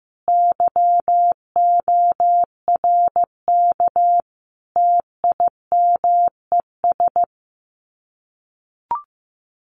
Morse code, heard as a keyed tone, about 15 wpm, 700 Hz. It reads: YORK TIMES
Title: Sets of 3 Words 15wpm